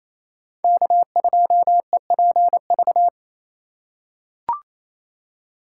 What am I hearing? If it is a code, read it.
Morse code, heard as a keyed tone, about 28 wpm, 700 Hz. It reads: K2EPV
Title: USA Call Signs - 5 Characters - 8x Character Spacing - 28wpm